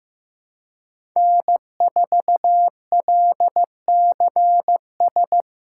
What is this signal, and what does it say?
Morse code, heard as a keyed tone, about 15 wpm, 700 Hz. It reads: N4LCS
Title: USA Callsign DE Callsign 15wpm